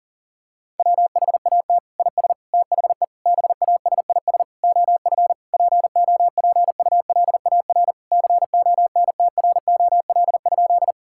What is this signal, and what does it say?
Morse code, heard as a keyed tone, about 40 wpm, 700 Hz. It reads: WHAT IS THE BASIS OF POPULAR CONTROL?
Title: Sentences from Top 1000 Words - Review Part 5 - 40wpm